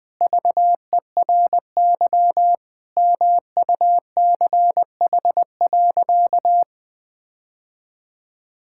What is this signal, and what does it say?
Morse code, heard as a keyed tone, about 20 wpm, 700 Hz. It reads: VERY MUCH.
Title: Sentences with 3-Word Phrases - Spoken Beginning - 20wpm